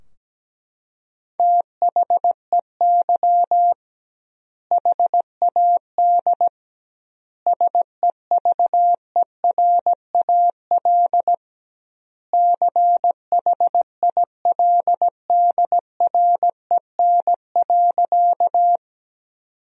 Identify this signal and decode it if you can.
Morse code, heard as a keyed tone, about 17 wpm, 700 Hz. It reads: THEY HAD SEVERAL CHILDREN.
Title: 2x Word-Spacing - Sentences from Top 400 Words - 17wpm